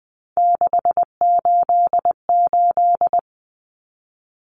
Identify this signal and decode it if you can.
Morse code, heard as a keyed tone, about 20 wpm, 700 Hz. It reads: 688
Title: Numbers - 3-Digit 20wpm